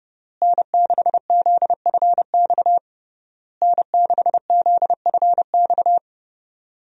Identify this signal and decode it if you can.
Morse code, heard as a keyed tone, about 30 wpm, 700 Hz. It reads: N6ZFX N6ZFX
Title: USA Call Signs - 5 Characters - Repeated 3x - 30wpm